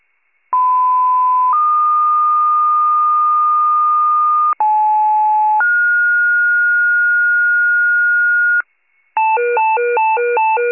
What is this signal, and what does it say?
Two-tone page, heard as a one-second tone, then a three-second tone, about 1010 then 1220 Hz, about 860 then 1430 Hz, with noise. It is heard from 0.5 to 8.6 s.